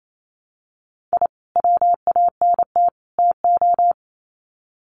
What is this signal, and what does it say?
Morse code, heard as a keyed tone, about 28 wpm, 700 Hz. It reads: I WANT TO
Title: Sets of 3 Words 28wpm